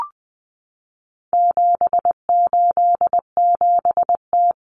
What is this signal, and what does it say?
Morse code, heard as a keyed tone, about 20 wpm, 700 Hz. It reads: 787T
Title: Common Cut Numbers - 4-Digit 20wpm